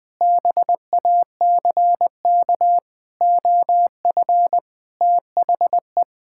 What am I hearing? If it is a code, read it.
Morse code, heard as a keyed tone, about 20 wpm, 700 Hz. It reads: BACK OF THE
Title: Sentences with 5-Word Phrases - 20wpm